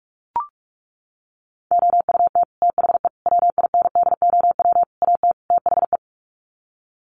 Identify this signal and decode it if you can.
Morse code, heard as a keyed tone, about 44 wpm, 700 Hz. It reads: OUT THE WINDOW AT THE
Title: Sets of 5 Words Encore2 45wpm